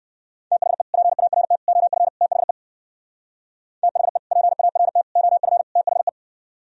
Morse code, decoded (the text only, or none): THE PART OF THE THE PART OF THE